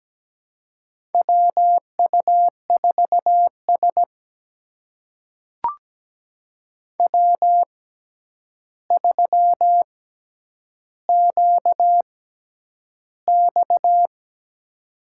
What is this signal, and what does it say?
Morse code, heard as a keyed tone, about 17 wpm, spaced as for 9 wpm, 700 Hz. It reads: WU4S W3QX
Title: USA Call Signs - 6x Character Spacing - 17wpm